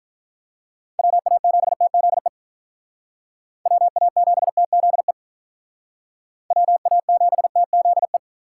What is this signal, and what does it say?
Morse code, heard as a keyed tone, about 43 wpm, 700 Hz. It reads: WA7TZE WA7TZE WA7TZE
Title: Speed Racing - USA Call Signs 35wpm